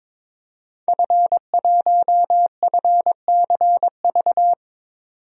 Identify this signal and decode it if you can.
Morse code, heard as a keyed tone, about 22 wpm, 700 Hz. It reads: F1FCV
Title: DXCC-100 Call Signs - 5 Characters - 22wpm